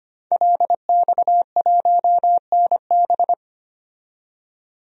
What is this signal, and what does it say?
Morse code, heard as a keyed tone, about 25 wpm, 700 Hz. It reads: LX1NB
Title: DXCC-100 Call Signs - 5 Characters - 25wpm